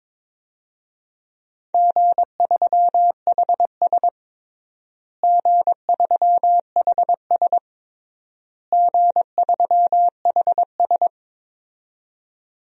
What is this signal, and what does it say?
Morse code, heard as a keyed tone, about 22 wpm, 700 Hz. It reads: G3HS G3HS G3HS
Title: DXCC-100 Call Signs - 4 Characters - Repeated 3x - 22wpm